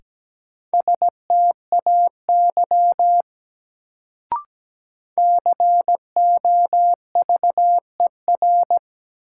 Morse code, heard as a keyed tone, about 17 wpm, 700 Hz. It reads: STAY COVER
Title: Sentences from Top 500 Words - New Word Drill - 17wpm